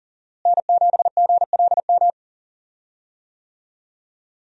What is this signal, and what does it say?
Morse code, heard as a keyed tone, about 40 wpm, 700 Hz. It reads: N7GLM
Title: USA Call Signs - 5 Characters - Repeated 2x - 40wpm